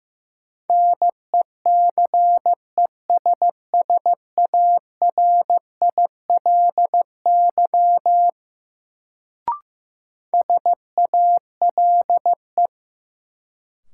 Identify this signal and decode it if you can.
Morse code, heard as a keyed tone, about 15 wpm, 700 Hz. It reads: NECESSARILY SALE
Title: Sentences from Top 1400 Words - New Word Drill - 15wpm